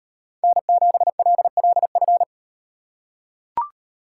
Morse code, heard as a keyed tone, about 38 wpm, 700 Hz. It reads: N7LLF